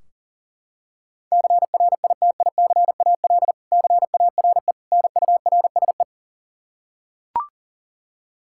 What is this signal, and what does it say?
Morse code, heard as a keyed tone, about 37 wpm, 700 Hz. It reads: CRITICAL CARE NURSE
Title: K1USN SST QSO Practice 40wpm